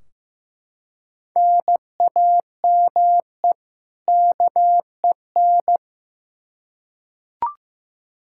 Morse code, heard as a keyed tone, about 15 wpm, 700 Hz. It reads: NAME KEN
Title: K1USN SST QSO Practice 15wpm